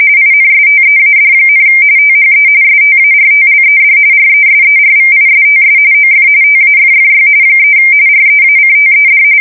RTTY, about 45.45 baud. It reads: INSTEAD, THE STARTLED RESERVISTS OF TAFFY 3 REACTED
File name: fsk170 / USS-New-Jersey